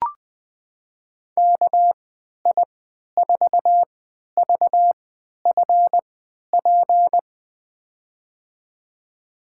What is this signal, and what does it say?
Morse code, heard as a keyed tone, about 20 wpm, spaced as for 11 wpm, 700 Hz. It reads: KI4VFP
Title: USA Call Signs - 6 Characters - 3x Character Spacing - 20wpm